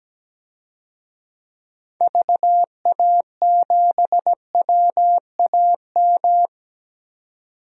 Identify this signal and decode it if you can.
Morse code, heard as a keyed tone, about 17 wpm, 700 Hz. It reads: VA7WAM
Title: DXCC-100 Call Signs - 6 Characters - 17wpm